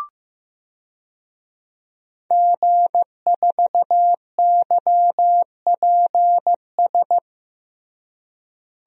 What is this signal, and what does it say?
Morse code, heard as a keyed tone, about 16 wpm, 700 Hz. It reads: G4YPS
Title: DXCC-100 Call Signs - 5 Characters - Repeated 2x - 15wpm